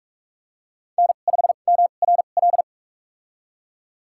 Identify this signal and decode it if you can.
Morse code, heard as a keyed tone, about 45 wpm, spaced as for 32 wpm, 700 Hz. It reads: N5MRL
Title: USA Call Signs - 5 Characters - 2x Character Spacing - 45wpm